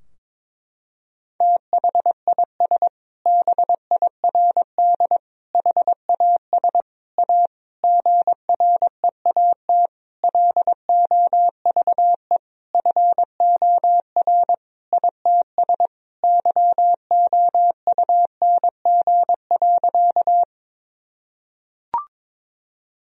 Morse code, heard as a keyed tone, about 22 wpm, 700 Hz. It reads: THIS BIRD HAS A GREAT LOVE FOR ITS YOUNG.
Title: Sentences from Top 1000 Words - Review Part 2 - 22wpm